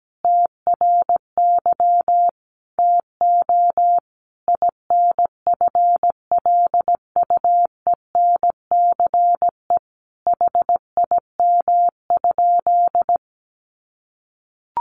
Morse code, heard as a keyed tone, about 17 wpm, 700 Hz. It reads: TRY TO INFLUENCE HIM?
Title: Sentences from Top 1100 Words 17wpm